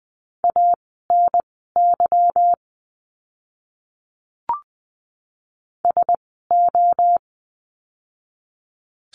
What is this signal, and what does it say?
Morse code, heard as a keyed tone, about 20 wpm, spaced as for 14 wpm, 700 Hz. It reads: ANY SO
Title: 230 Lesson 37f - Letter x - Words 20wpm